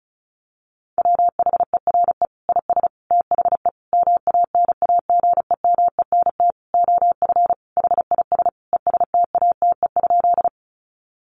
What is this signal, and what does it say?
Morse code, heard as a keyed tone, about 34 wpm, 700 Hz. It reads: WHERE IS THE MANAGEMENT OF HIS ESTATE?
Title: Sentences from Top 1200 Words 35wpm